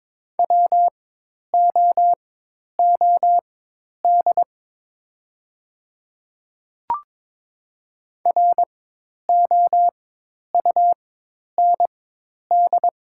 Morse code, heard as a keyed tone, about 22 wpm, spaced as for 10 wpm, 700 Hz. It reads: WOOD ROUND
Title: 4x Character Spacing - Top 300 Words - 22wpm